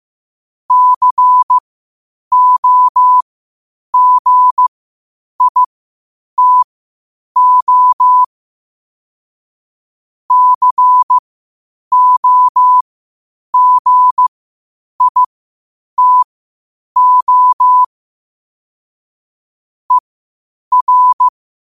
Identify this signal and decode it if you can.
Morse code, heard as a keyed tone, about 15 wpm, spaced as for 8 wpm, 1000 Hz. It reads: COGITO COGITO ER
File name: qotdfiles / 8wpm